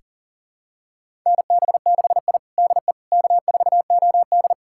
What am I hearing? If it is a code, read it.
Morse code, heard as a keyed tone, about 40 wpm, 700 Hz. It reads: NB6I DE K4OD
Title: USA Callsign DE Callsign - 4 Characters - 40wpm